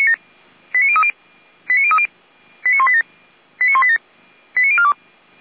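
Five-tone selective call (ZVEI), heard as quick runs of five tones, tones of about 70 ms. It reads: 89030 89030 89187 89187 89042
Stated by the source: ZVEI five-tone selective call received by SDR.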